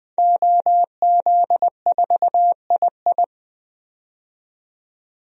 Morse code, heard as a keyed tone, about 20 wpm, 700 Hz. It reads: OZ4II
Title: DXCC-100 Call Signs - 5 Characters - 20wpm